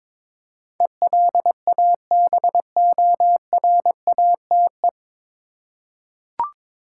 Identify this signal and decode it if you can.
Morse code, heard as a keyed tone, about 21 wpm, 700 Hz. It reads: ELABORATE
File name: 9 Letter Words - First 3 Letters Given - 22wpm